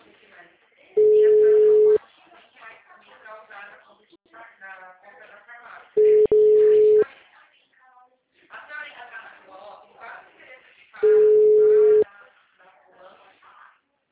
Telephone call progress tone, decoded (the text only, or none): ringback tone